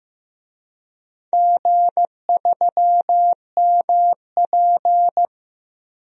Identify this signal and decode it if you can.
Morse code, heard as a keyed tone, about 15 wpm, 700 Hz. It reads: G3MP
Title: DXCC-100 Call Signs - 4 Characters - Repeated 2x - 15wpm